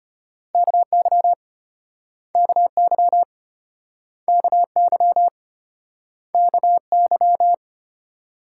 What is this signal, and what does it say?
Morse code, heard as a keyed tone, about 36 wpm, 700 Hz. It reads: KY KY KY KY